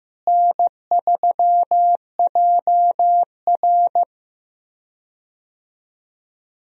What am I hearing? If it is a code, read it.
Morse code, heard as a keyed tone, about 15 wpm, 700 Hz. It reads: N3JR
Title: USA Call Signs - 4 Characters - Repeated 3x - 15wpm